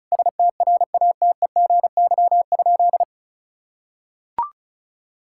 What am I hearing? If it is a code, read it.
Morse code, heard as a keyed tone, about 36 wpm, 700 Hz. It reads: STRATEGY?
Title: Sentences from Top 1000 Words - Review Part 6 - 35wpm